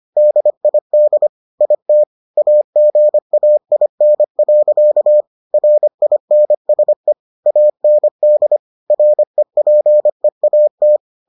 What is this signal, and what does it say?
Morse code, heard as a keyed tone, about 25 wpm, 600 Hz. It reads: DID IT AGAIN. RINSE AND REPEAT